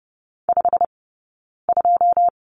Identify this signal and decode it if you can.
Morse code, heard as a keyed tone, about 30 wpm, spaced as for 9 wpm, 690 Hz. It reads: 52
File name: course / 30:10wpm / 086 Lesson 15d - Number 5 - Two Characters 30wpm